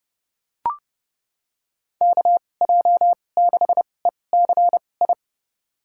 Morse code, heard as a keyed tone, about 32 wpm, spaced as for 22 wpm, 700 Hz. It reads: KJ6ECI
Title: USA Call Signs - 6 Characters - 2x Character Spacing - 30wpm